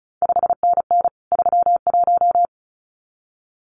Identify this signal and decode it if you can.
Morse code, heard as a keyed tone, about 36 wpm, 700 Hz. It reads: 5NN 31
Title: CQ WW DX Contest Exchange Encore - 35wpm